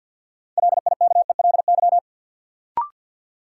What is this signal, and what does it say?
Morse code, heard as a keyed tone, about 49 wpm, 700 Hz. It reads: LIKELY
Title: Sets of 2 Words - New Word Drill - 50wpm